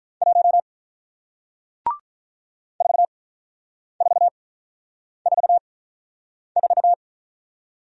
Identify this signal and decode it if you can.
Morse code, heard as a keyed tone, about 47 wpm, spaced as for 9 wpm, 700 Hz. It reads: 1 4444